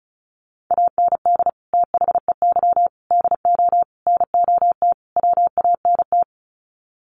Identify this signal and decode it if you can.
Morse code, heard as a keyed tone, about 34 wpm, 700 Hz. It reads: AND THEY DO NOT WANT
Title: Sets of 5 Words Encore2 35wpm